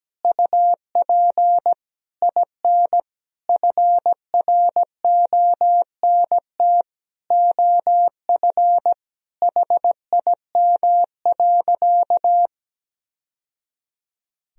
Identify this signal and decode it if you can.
Morse code, heard as a keyed tone, about 17 wpm, 700 Hz. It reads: UP IN FRONT OF HIM.
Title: Sentences from Top 1000 Words - Review Part 2 - 17wpm